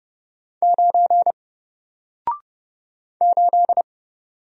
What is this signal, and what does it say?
Morse code, heard as a keyed tone, about 30 wpm, 700 Hz. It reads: 9 8
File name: Single-Letter-Number-30wpm